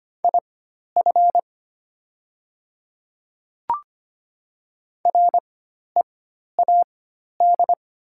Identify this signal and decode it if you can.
Morse code, heard as a keyed tone, about 25 wpm, spaced as for 12 wpm, 700 Hz. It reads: IF READ
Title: 4x Character Spacing - Top 300 Words - 25wpm